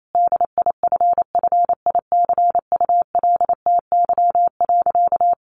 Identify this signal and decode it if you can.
Morse code, heard as a keyed tone, about 28 wpm, 700 Hz. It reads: DIFFICULTY.
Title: Sentences from Top 1500 Words 28wpm